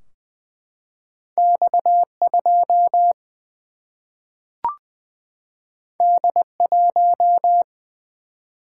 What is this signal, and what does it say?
Morse code, heard as a keyed tone, about 21 wpm, 700 Hz. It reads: X 2 D 1
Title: Single-Letter Single-Number 20wpm